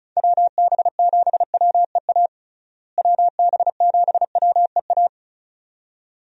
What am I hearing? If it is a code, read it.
Morse code, heard as a keyed tone, about 36 wpm, 700 Hz. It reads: WB7WEA WB7WEA
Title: USA Call Signs - 6 Characters - Repeated 2x - 35wpm